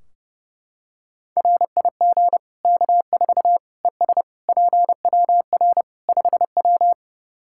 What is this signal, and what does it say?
Morse code, heard as a keyed tone, about 31 wpm, 700 Hz. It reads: RIG K4 ES PWR 5W
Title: K1USN SST QSO Practice 30wpm